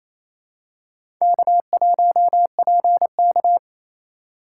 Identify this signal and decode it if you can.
Morse code, heard as a keyed tone, about 28 wpm, 700 Hz. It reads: K1PK
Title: USA Call Signs - 6x Character Spacing - 28wpm